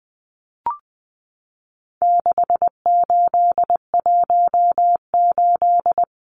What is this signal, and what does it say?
Morse code, heard as a keyed tone, about 20 wpm, 700 Hz. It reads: 6818
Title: Numbers - 4-Digit 20wpm